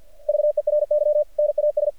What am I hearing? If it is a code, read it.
Morse code, heard as a keyed tone, about 48 wpm, 600 Hz. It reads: VERY NAS
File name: archived / CRSnet